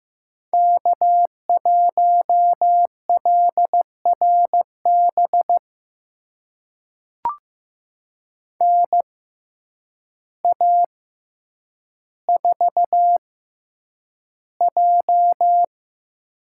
Morse code, heard as a keyed tone, about 15 wpm, spaced as for 8 wpm, 700 Hz. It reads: K1LRB NA4J
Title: USA Call Signs - 6x Character Spacing - 15wpm